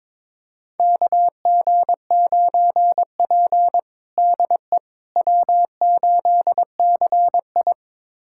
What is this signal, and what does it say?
Morse code, heard as a keyed tone, about 22 wpm, 700 Hz. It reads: KG9P DE W8CI
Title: USA Callsign DE Callsign - 4 Characters - 22wpm